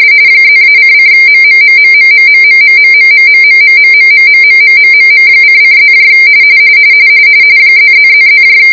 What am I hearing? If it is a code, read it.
RTTY, about 45.45 baud. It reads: +-::MMMMMMMMMMMMMMMMMMMMMMMM::+ ++++